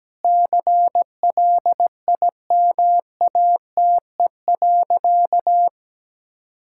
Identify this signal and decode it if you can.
Morse code, heard as a keyed tone, about 17 wpm, 700 Hz. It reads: CLIMATE.
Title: Sentences from Top 1600 Words 17wpm